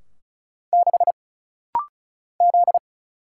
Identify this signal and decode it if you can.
Morse code, heard as a keyed tone, about 36 wpm, 700 Hz. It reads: 6 Z
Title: Single Letter-Number - Rapid-Fire 35wpm